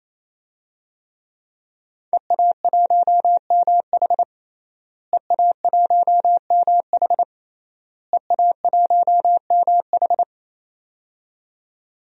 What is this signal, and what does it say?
Morse code, heard as a keyed tone, about 27 wpm, 700 Hz. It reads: EA1MH EA1MH EA1MH
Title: DXCC-100 Call Signs - 5 Characters - Repeated 3x - 28wpm